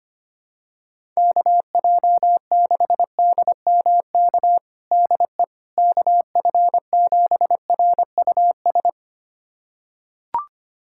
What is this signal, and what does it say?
Morse code, heard as a keyed tone, about 25 wpm, 700 Hz. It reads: KJ6DMK DE KF7RUS
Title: USA Callsign DE Callsign - 6 Characters - 25wpm